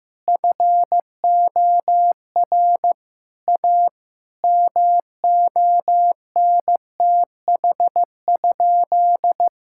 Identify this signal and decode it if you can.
Morse code, heard as a keyed tone, about 15 wpm, 700 Hz. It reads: FOR A MONTH?
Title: Sentences from Top 1000 Words - Review Part 5 - 15wpm